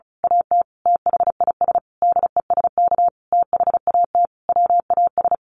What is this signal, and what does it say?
Morse code, heard as a keyed tone, about 34 wpm, 700 Hz. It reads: AT THIS DESK THAT WAS
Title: A Sound Of Thunder 35wpm